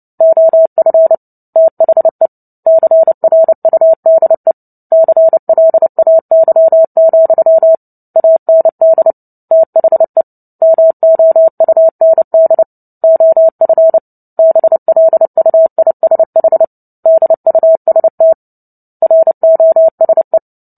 Morse code, heard as a keyed tone, about 30 wpm, 650 Hz. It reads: OF THE CRUDE CLAY, AND THE MOUND OF BLUISH DUST ROSE
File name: Book Two / 03-Chapter 3 - 29 WPM